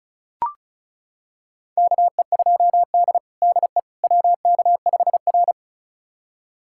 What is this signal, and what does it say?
Morse code, heard as a keyed tone, about 36 wpm, 700 Hz. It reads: KE2D DE WK5R
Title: USA Callsign DE Callsign - 4 Characters - 35wpm